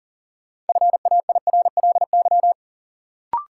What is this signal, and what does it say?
Morse code, heard as a keyed tone, about 37 wpm, 700 Hz. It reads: FAIRLY
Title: Sentences from Top 1100 Words - New Word Drill Alt - 40wpm